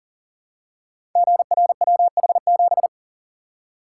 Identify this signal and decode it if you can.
Morse code, heard as a keyed tone, about 39 wpm, 700 Hz. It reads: GRWH7
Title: Code Groups - 5 Letter:Numbers - 40wpm